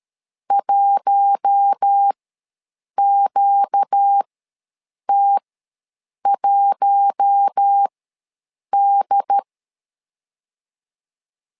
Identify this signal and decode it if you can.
Morse code, heard as a keyed tone, about 13 wpm, spaced as for 7 wpm, 800 Hz. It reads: 1QT1D
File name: LEZIONI / 4 Q1 ws 12wpm 2129